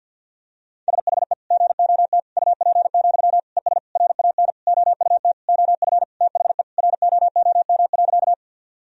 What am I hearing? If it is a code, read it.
Morse code, heard as a keyed tone, about 52 wpm, 700 Hz. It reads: SHE GOT UP, ES RAN OUT OF THE ROOM.
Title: Sentences from Top 500 Words and Abbreviations 50wpm